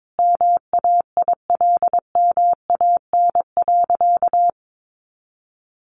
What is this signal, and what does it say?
Morse code, heard as a keyed tone, about 22 wpm, 700 Hz. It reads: MAILMAN.
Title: Additional Easy Phrases 22wpm